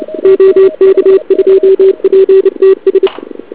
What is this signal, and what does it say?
Morse code, heard as a keyed tone, about 29 wpm, 380 Hz. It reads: OK2PTS
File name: ARCHIV / KX3 / CW pileup